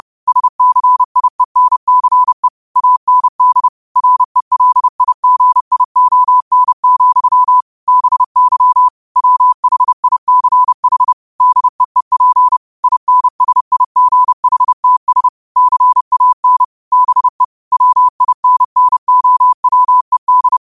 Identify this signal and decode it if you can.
Morse code, heard as a keyed tone, about 30 wpm, 1000 Hz. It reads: SCIENCE AND RELIGION, BY WHICH DEEP INSIGHTS CAN BE WINNOWED
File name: qotdfiles / 30wpm